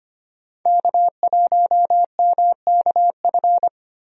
Morse code, heard as a keyed tone, about 25 wpm, 690 Hz. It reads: K1MKF
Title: USA Call Signs - 5 Characters - 25wpm